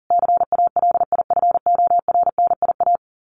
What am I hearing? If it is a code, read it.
Morse code, heard as a keyed tone, about 40 wpm, 700 Hz. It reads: CALIFORNIA
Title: Sets of 5 Words Encore2 40wpm